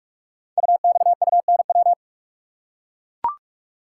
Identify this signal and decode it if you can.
Morse code, heard as a keyed tone, about 44 wpm, 700 Hz. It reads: UXUNW
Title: Code Groups - 5 Letters - 45wpm